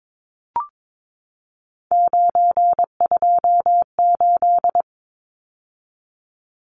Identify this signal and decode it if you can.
Morse code, heard as a keyed tone, about 22 wpm, 700 Hz. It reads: 928
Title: Numbers - 3-Digit 22wpm